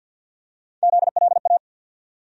Morse code, heard as a keyed tone, about 53 wpm, 700 Hz. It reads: ZLA
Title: USA Call Sign Postfixes - 3 Letters 50wpm